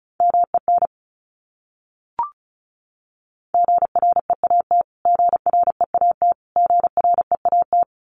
Morse code, heard as a keyed tone, about 34 wpm, 700 Hz. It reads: MEN GREAT GREAT GREAT
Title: Top 300 Words Repeated 35wpm